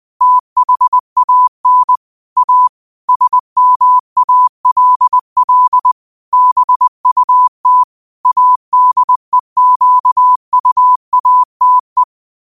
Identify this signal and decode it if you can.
Morse code, heard as a keyed tone, about 20 wpm, 1000 Hz. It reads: THAN A SMALL BUT ADEQUATE
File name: qotdfiles / 20wpm